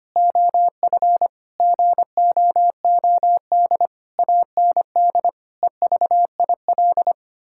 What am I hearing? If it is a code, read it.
Morse code, heard as a keyed tone, about 25 wpm, 700 Hz. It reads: OF GOOD AND EVIL
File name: Sets of 9 Words - 25wpm